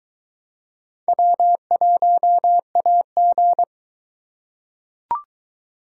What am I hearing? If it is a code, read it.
Morse code, heard as a keyed tone, about 22 wpm, 700 Hz. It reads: W1AG